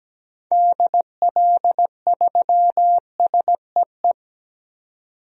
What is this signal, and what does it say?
Morse code, heard as a keyed tone, about 17 wpm, 700 Hz. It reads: DL3SEE
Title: DXCC-100 Call Signs - 6 Characters - 17wpm